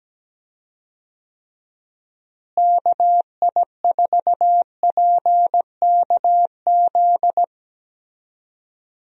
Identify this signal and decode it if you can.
Morse code, heard as a keyed tone, about 17 wpm, 700 Hz. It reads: KI4PKZ
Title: USA Call Signs - 6 Characters - Repeated 3x - 17wpm